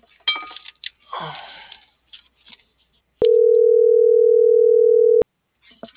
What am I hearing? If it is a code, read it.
Ringback tone (North America), 440 + 480 Hz.